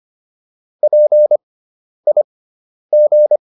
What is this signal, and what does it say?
Morse code, heard as a keyed tone, about 25 wpm, spaced as for 10 wpm, 600 Hz. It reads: PIG